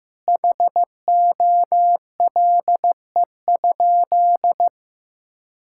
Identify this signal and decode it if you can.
Morse code, heard as a keyed tone, about 16 wpm, 700 Hz. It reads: HOLE?
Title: Sentences from Top 1600 Words 15wpm